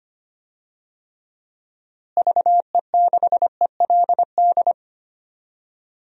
Morse code, heard as a keyed tone, about 25 wpm, 700 Hz. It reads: VE6ELD
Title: DXCC-100 Call Signs - 6 Characters - Repeated 3x - 25wpm